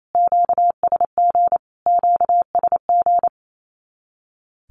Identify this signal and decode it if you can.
Morse code, heard as a keyed tone, about 28 wpm, 700 Hz. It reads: QSG QSG
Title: Q-Codes Repeated 3x 28wpm